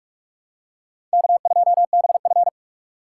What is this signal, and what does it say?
Morse code, heard as a keyed tone, about 45 wpm, 700 Hz. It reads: K2BF
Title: USA Call Signs - 4 Characters - 4x Character Spacing - 45wpm